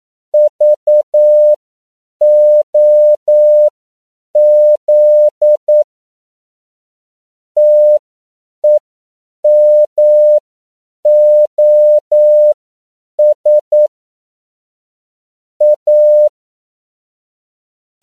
Morse code, heard as a keyed tone, about 9 wpm, spaced as for 7 wpm, 600 Hz. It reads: VOZ TEMOS A
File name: CW / 10PPM